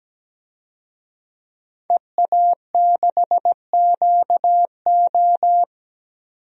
Morse code, heard as a keyed tone, about 17 wpm, 700 Hz. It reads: EA6QO
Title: DXCC-100 Call Signs - 5 Characters - 17wpm